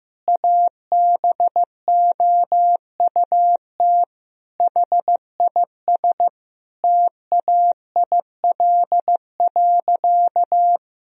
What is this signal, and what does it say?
Morse code, heard as a keyed tone, about 15 wpm, 700 Hz. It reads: ABOUT HIS TAIL.
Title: Sentences from Top 1000 Words - Review Part 3 - 15wpm